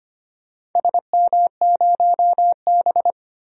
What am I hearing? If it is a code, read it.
Morse code, heard as a keyed tone, about 25 wpm, 700 Hz. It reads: SM0B
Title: DXCC-100 Call Signs - 4 Characters - 25wpm